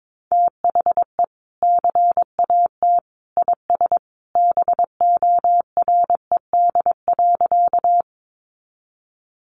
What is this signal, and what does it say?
Morse code, heard as a keyed tone, about 22 wpm, 700 Hz. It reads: THE CAT IS BORED.
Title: Easy Phrases 22wpm